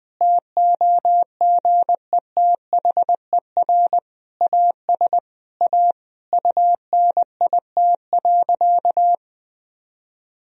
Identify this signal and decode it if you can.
Morse code, heard as a keyed tone, about 20 wpm, 700 Hz. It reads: TOGETHER AS A UNIT.
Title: Sentences from Top 1000 Words - Review Part 2 - 20wpm